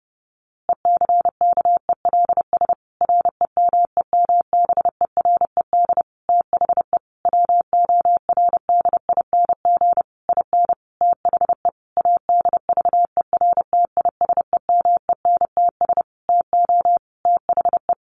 Morse code, heard as a keyed tone, about 30 wpm, 700 Hz. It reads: ECKELS REMEMBERED THE WORDING IN THE ADVERTISEMENTS TO THE
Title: A Sound Of Thunder 30wpm